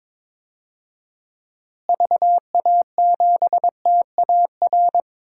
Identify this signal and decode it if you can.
Morse code, heard as a keyed tone, about 22 wpm, 700 Hz. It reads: VA7TAR
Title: DXCC-100 Call Signs - 6 Characters - Repeated 3x - 22wpm